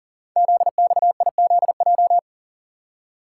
Morse code, heard as a keyed tone, about 41 wpm, 700 Hz. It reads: ZXIZJ
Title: Code Groups - 5 Letters - 40wpm